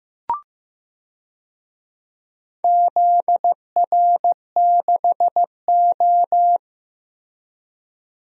Morse code, heard as a keyed tone, about 15 wpm, 700 Hz. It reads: ZR6O
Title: DXCC-100 Call Signs - 4 Characters - 15wpm